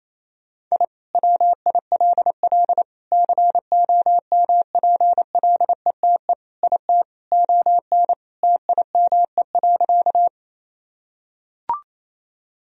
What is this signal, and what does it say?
Morse code, heard as a keyed tone, about 28 wpm, 700 Hz. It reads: I WILL COMPLETE IT ON TIME.
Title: Sentences from Top 1000 Words - Review Part 2 - 28wpm